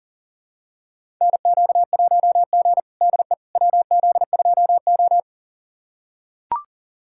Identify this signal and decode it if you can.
Morse code, heard as a keyed tone, about 40 wpm, 700 Hz. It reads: NQ1G DE WZ2O
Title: USA Callsign DE Callsign - 4 Characters - 40wpm